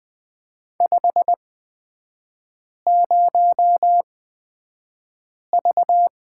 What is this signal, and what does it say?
Morse code, heard as a keyed tone, about 20 wpm, spaced as for 5 wpm, 700 Hz. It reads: 50V
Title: 229 Lesson 37e - Letter x - Three Characters 20wpm